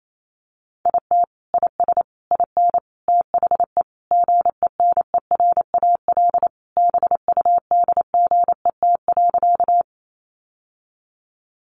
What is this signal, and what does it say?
Morse code, heard as a keyed tone, about 28 wpm, 700 Hz. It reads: IT IS IN THE GENERAL BUDGET.
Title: Sentences from Top 1200 Words 28wpm